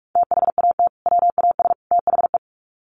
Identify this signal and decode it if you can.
Morse code, heard as a keyed tone, about 44 wpm, 700 Hz. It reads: THAT WAS THE
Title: Sets of 3 Words 45wpm